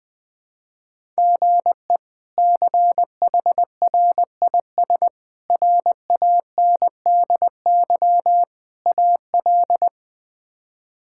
Morse code, heard as a keyed tone, about 20 wpm, 700 Hz. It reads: GE CHRIS RANDY AL
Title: K1USN SST QSO Practice - 20wpm